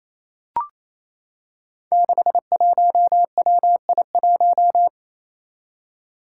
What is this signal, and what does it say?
Morse code, heard as a keyed tone, about 28 wpm, 700 Hz. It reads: 61WI1
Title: Five Challenging Letter-Number Combinations 28wpm